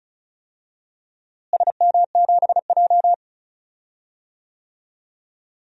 Morse code, heard as a keyed tone, about 34 wpm, 700 Hz. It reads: SM7J
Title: DXCC-100 Call Signs - 4 Characters - Repeated 3x - 35wpm